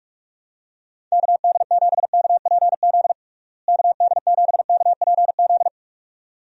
Morse code, heard as a keyed tone, about 46 wpm, 700 Hz. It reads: KD7KPZ KD7KPZ
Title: USA Call Signs - 6 Characters - Repeated 2x - 45wpm